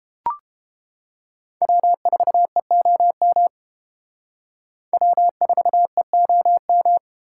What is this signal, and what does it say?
Morse code, heard as a keyed tone, about 30 wpm, 700 Hz. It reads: W4EOM W4EOM